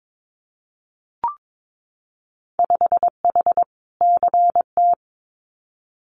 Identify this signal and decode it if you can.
Morse code, heard as a keyed tone, about 22 wpm, 700 Hz. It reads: 5H CT
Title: Winter Field Day Exchanges Encore 22wpm